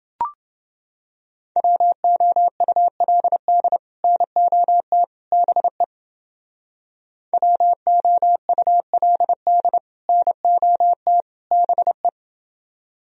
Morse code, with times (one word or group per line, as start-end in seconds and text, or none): 1.6-3.8 WOULD
4.0-5.0 NOT
5.3-5.8 BE
7.3-9.8 WOULD
10.1-11.2 NOT
11.5-12.1 BE